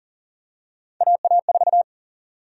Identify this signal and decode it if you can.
Morse code, heard as a keyed tone, about 40 wpm, 700 Hz. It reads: AA4
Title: USA Call Sign Prefixes - Repeated 2x - 40wpm